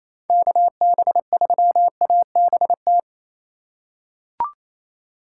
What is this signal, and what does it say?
Morse code, heard as a keyed tone, about 28 wpm, 700 Hz. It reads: KB3ABT
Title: USA Call Signs - 6 Characters - 6x Character Spacing - 28wpm